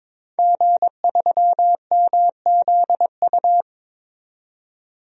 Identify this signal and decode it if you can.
Morse code, heard as a keyed tone, about 22 wpm, 700 Hz. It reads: G3MZU
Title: DXCC-100 Call Signs - 5 Characters - 22wpm